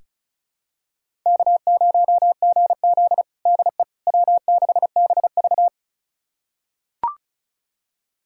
Morse code, heard as a keyed tone, about 36 wpm, 700 Hz. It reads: K0GZ DE W6BV
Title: USA Callsign DE Callsign - 4 Characters - 35wpm